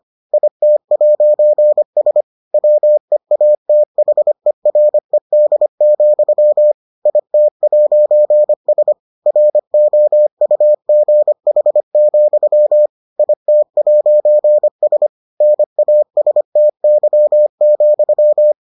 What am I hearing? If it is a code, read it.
Morse code, heard as a keyed tone, about 25 wpm, 600 Hz. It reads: IT'S WEATHERED, IT'S ROUGH, IT'S NASTY,